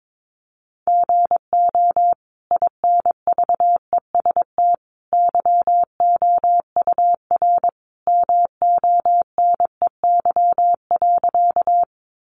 Morse code, heard as a keyed tone, about 22 wpm, 700 Hz. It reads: GO INVEST YOUR MONEY.
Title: Sentences from Top 1100 Words 22wpm